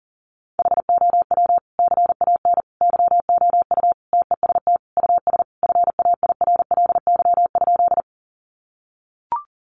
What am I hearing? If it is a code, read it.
Morse code, heard as a keyed tone, about 37 wpm, 700 Hz. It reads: HOW CAN YOU TEST US FAIRLY?
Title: Sentences from Top 1100 Words 40wpm